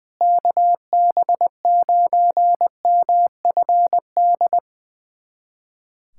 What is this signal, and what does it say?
Morse code, heard as a keyed tone, about 20 wpm, 700 Hz. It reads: KB9MFD
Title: USA Callsign DE Callsign - 6 Characters - 20wpm